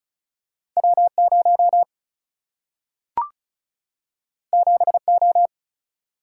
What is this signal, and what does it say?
Morse code, heard as a keyed tone, about 36 wpm, 700 Hz. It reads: W 0 7 O
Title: Single-Letter Single-Number 35wpm